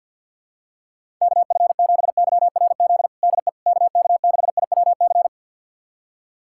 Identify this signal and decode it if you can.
Morse code, heard as a keyed tone, about 46 wpm, 700 Hz. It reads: KF7YRZ DE KK6IWC
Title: USA Callsign DE Callsign - 6 Characters - 50wpm